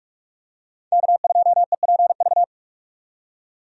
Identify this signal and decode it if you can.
Morse code, heard as a keyed tone, about 46 wpm, 700 Hz. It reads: K2EPV
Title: USA Call Signs - 5 Characters - 45wpm